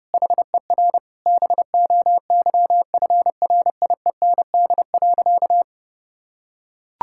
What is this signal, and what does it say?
Morse code, heard as a keyed tone, about 30 wpm, 700 Hz. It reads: HER BOYFRIEND.
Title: Sentences from Top 1000 Words - Review Part 5 - 30wpm